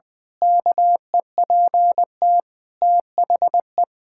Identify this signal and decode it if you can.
Morse code, heard as a keyed tone, about 20 wpm, 700 Hz. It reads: KEPT THE
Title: Pangram Challenge v2 - 20wpm